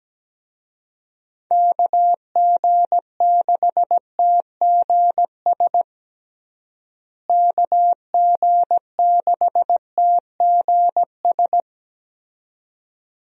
Morse code, heard as a keyed tone, about 17 wpm, 700 Hz. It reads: KG6TGS KG6TGS
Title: USA Call Signs - 6 Characters - Repeated 2x - 17wpm